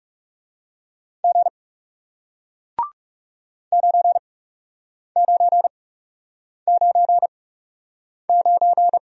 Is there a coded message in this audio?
Yes: Morse code from 1.2 to 9.0 s, heard as a keyed tone, about 40 wpm, spaced as for 8 wpm, 700 Hz. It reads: G 9999